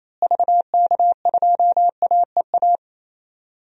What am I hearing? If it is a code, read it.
Morse code, heard as a keyed tone, about 28 wpm, 700 Hz. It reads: VK2AEA
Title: DXCC-100 Call Signs - 6 Characters - Repeated 2x - 28wpm